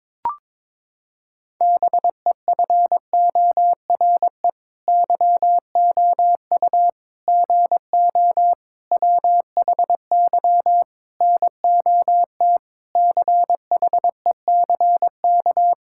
Morse code, heard as a keyed tone, about 22 wpm, 700 Hz. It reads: BEFORE YOU GO WHY NOT CHECK
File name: Sets of 9 Words - 22wpm